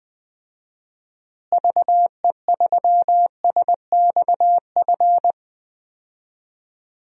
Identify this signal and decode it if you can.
Morse code, heard as a keyed tone, about 20 wpm, 700 Hz. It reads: VE3SXF
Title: DXCC-100 Call Signs - 6 Characters - Repeated 2x - 20wpm